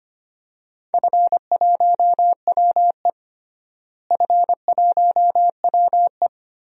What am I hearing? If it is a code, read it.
Morse code, heard as a keyed tone, about 25 wpm, 700 Hz. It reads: F1WE F1WE
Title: DXCC-100 Call Signs - 4 Characters - Repeated 2x - 25wpm